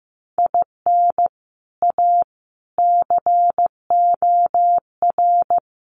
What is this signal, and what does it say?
Morse code, heard as a keyed tone, about 15 wpm, 700 Hz. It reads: IN A COR
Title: Sets of 5 Words Encore2 15wpm